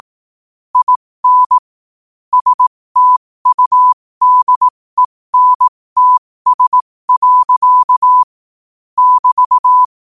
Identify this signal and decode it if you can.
Morse code, heard as a keyed tone, about 18 wpm, spaced as for 15 wpm, 1000 Hz. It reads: IN STUDENTS. <BT>
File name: qotdfiles / 15wpm